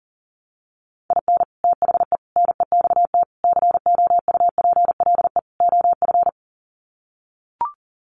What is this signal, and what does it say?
Morse code, heard as a keyed tone, about 40 wpm, 700 Hz. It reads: IN THE NEXT COUPLE OF
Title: Sets of 5 Words Encore2 40wpm